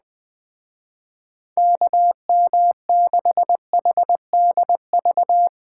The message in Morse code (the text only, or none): KM6HDV